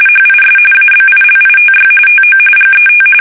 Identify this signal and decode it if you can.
RTTY, about 45.45 baud. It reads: HIS ABILITY TO GET
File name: fsk850 / benjamin-g-lamme